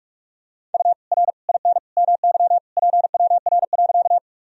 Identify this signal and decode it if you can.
Morse code, heard as a keyed tone, about 46 wpm, 700 Hz. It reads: U R IN MY PWR.
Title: Sentences from Top 500 Words and Abbreviations 45wpm